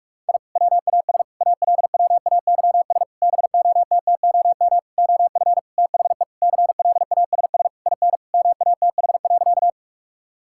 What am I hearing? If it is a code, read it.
Morse code, heard as a keyed tone, about 41 wpm, 700 Hz. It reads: I WAS ALWAYS BOTTOM OF THE CLASS IN MATH.
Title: Sentences from Top 1000 Words - Review Part 6 - 45wpm